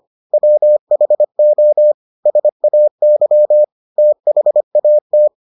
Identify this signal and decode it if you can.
Morse code, heard as a keyed tone, about 25 wpm, 600 Hz. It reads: WHO SAY THAT